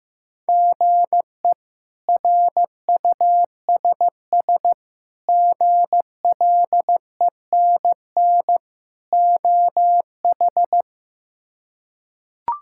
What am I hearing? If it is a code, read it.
Morse code, heard as a keyed tone, about 16 wpm, 700 Hz. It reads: GE RUSS GLENN OH
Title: K1USN SST QSO Practice - 15wpm